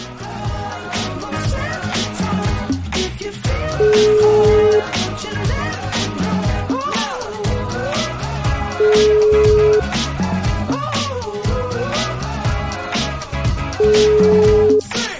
Ringback tone (Europe, 425 Hz), with noise.